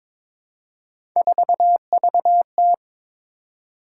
Morse code, heard as a keyed tone, about 22 wpm, 700 Hz. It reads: 4VT
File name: practice / 22wpm / Three Letter-Number Combinations - 22wpm